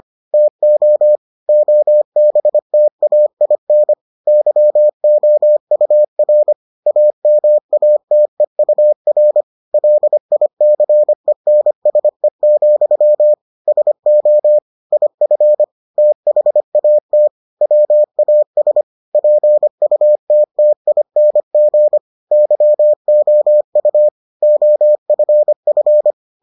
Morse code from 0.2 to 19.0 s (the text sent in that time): TO OBTAIN YOUR AMATEUR LICENSE, SO IF THAT WAS